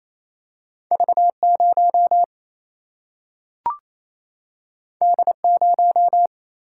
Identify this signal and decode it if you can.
Morse code, heard as a keyed tone, about 28 wpm, 700 Hz. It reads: V 0 D 0
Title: Single-Letter Single-Number 28wpm